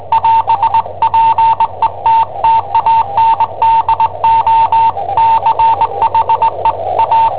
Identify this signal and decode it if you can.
Morse code, heard as a keyed tone, about 20 wpm, 900 Hz, with radio noise. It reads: <AS>PETTANDO CHE A
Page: Cassetta esercizi preparata da I7OHP per IW5ELP